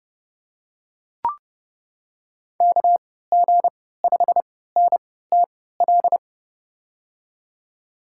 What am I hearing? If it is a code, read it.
Morse code, heard as a keyed tone, about 30 wpm, spaced as for 17 wpm, 700 Hz. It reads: KG5NTL
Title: USA Call Signs - 6 Characters - 3x Character Spacing - 30wpm